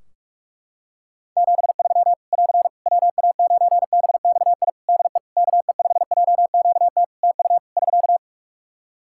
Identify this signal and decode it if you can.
Morse code, heard as a keyed tone, about 41 wpm, 700 Hz. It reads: 73 <AR> WA9BXI DE KE5JQT TU <SK>
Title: Practice QSO - Exchange 4 - Ending - 45wpm